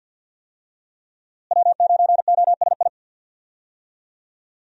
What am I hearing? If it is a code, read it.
Morse code, heard as a keyed tone, about 52 wpm, 700 Hz. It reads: W9OSI
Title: USA Call Signs - 5 Characters - Repeated 3x - 50wpm